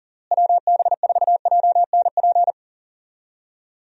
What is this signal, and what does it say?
Morse code, heard as a keyed tone, about 40 wpm, 700 Hz. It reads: WB4JNP
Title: USA Call Signs - 6 Characters - Repeated 3x - 40wpm